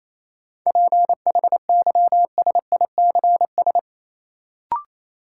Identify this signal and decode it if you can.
Morse code, heard as a keyed tone, about 27 wpm, 700 Hz. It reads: PHYSICS
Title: Sentences from Top 1500 Words - New Word Drill Alt - 28wpm